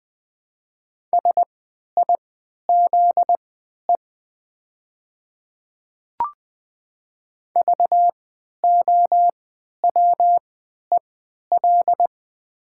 Morse code, heard as a keyed tone, about 20 wpm, spaced as for 11 wpm, 700 Hz. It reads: SIZE VOWEL
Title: 3x Character Spacing - Top 500 Words - 20wpm